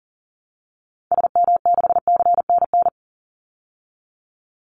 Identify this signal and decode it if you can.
Morse code, heard as a keyed tone, about 40 wpm, 700 Hz. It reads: SM6CNN
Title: Call Signs Repeated 3x - 40wpm